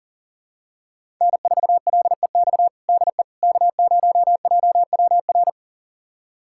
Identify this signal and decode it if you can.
Morse code, heard as a keyed tone, about 40 wpm, 700 Hz. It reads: N4LEX DE K0JWR
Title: USA Callsign DE Callsign - 5 Characters - 40wpm